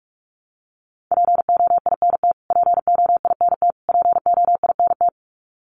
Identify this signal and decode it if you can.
Morse code, heard as a keyed tone, about 44 wpm, 700 Hz. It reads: POINT POINT POINT
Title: Top 300 Words Repeated 45wpm